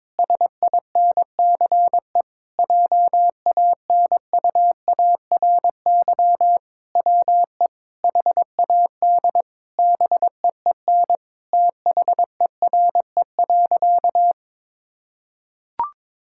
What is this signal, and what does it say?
Morse code, heard as a keyed tone, about 21 wpm, 700 Hz. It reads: SINCE JANUARY WE HAD BEEN THERE.
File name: practice / 22wpm / Sentences with 4-Word Phrases - 22wpm